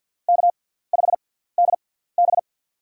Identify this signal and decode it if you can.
Morse code, heard as a keyed tone, about 53 wpm, spaced as for 17 wpm, 700 Hz. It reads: K5DB
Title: USA Call Signs - 4 Characters - 6x Character Spacing - 50wpm